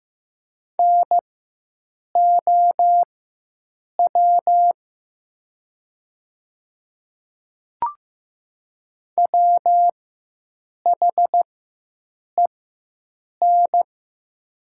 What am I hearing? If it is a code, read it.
Morse code, heard as a keyed tone, about 15 wpm, spaced as for 7 wpm, 700 Hz. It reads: NOW WHEN
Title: 4x Character Spacing - Top 100 Words - 15wpm